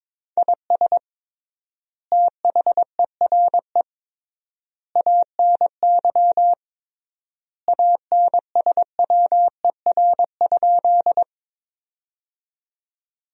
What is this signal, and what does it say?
Morse code, heard as a keyed tone, about 23 wpm, 700 Hz. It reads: IS THERE ANY ANSWER?
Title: 3x Word-Spacing - Sentences from Top 300 Words - 22wpm